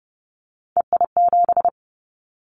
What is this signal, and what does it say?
Morse code, heard as a keyed tone, about 30 wpm, 700 Hz. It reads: EI7
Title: Callsign Prefixes for Top 50 DXCC - 30wpm